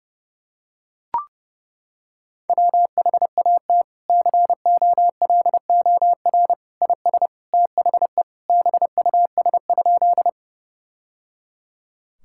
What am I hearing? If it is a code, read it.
Morse code, heard as a keyed tone, about 30 wpm, 700 Hz. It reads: WHAT COLOR IS THE BUS?
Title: Sentences from Top 1000 Words - Review Part 6 - 30wpm